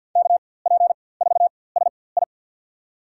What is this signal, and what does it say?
Morse code, heard as a keyed tone, about 53 wpm, spaced as for 24 wpm, 700 Hz. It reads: KP4SI
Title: USA Call Signs - 5 Characters - 4x Character Spacing - 50wpm